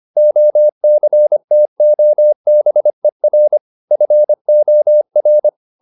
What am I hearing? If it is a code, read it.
Morse code, heard as a keyed tone, about 25 wpm, 600 Hz. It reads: OCTOBER FOR
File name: the old man and the sea part 1 25wpm